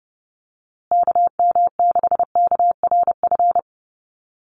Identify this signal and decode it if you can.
Morse code, heard as a keyed tone, about 30 wpm, 700 Hz. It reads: KM6KRF
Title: USA Call Signs Repeated 3x 30wpm